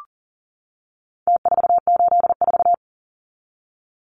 Morse code, heard as a keyed tone, about 40 wpm, 700 Hz. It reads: T484
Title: Numbers - Contest Style - 4-Digit 40wpm